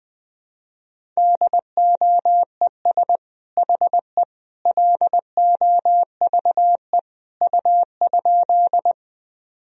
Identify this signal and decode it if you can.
Morse code, heard as a keyed tone, about 20 wpm, 700 Hz. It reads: DOES HE LOVE U?
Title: Sentences from Top 400 Words and Abbreviations 20wpm